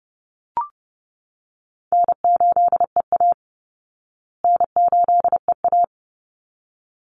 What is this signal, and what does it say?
Morse code, heard as a keyed tone, about 30 wpm, 700 Hz. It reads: N8EA N8EA
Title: Call Signs Repeated 3x - 30wpm